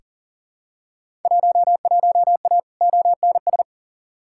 Morse code, heard as a keyed tone, about 38 wpm, 700 Hz. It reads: 11A ONS
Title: Field Day Exchanges Encore - 40wpm